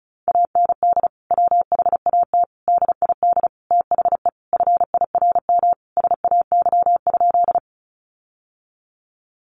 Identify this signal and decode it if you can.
Morse code, heard as a keyed tone, about 34 wpm, 700 Hz. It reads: AND WHAT DID THE FIRM SAY?
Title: Sentences from Top 1400 Words 35wpm